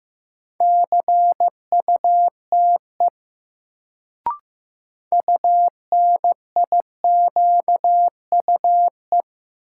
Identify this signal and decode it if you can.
Morse code, heard as a keyed tone, about 15 wpm, 700 Hz. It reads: CUTE UNIQUE
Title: Sentences from Top 1000 Words - Review Part 7 - New Word Drill - 15wpm